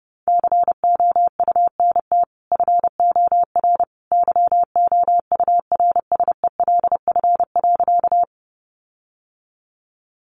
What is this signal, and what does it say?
Morse code, heard as a keyed tone, about 30 wpm, 700 Hz. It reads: COUNT FOR YOURSELF.
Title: Sentences from Top 1500 Words 30wpm